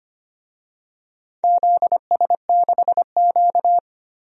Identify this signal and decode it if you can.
Morse code, heard as a keyed tone, about 25 wpm, 700 Hz. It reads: ZS6Q
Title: DXCC-100 Call Signs - 4 Characters - 25wpm